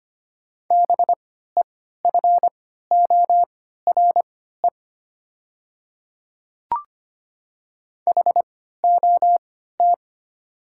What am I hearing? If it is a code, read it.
Morse code, heard as a keyed tone, about 25 wpm, spaced as for 14 wpm, 700 Hz. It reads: BEFORE HOT
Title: 3x Character Spacing - Top 200 Words - 25wpm